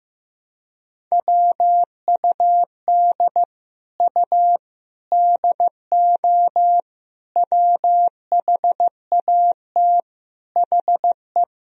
Morse code, heard as a keyed tone, about 15 wpm, 700 Hz. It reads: WUD U DO WHAT HE
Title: Sentences from Top 100 Words and Abbreviations 15wpm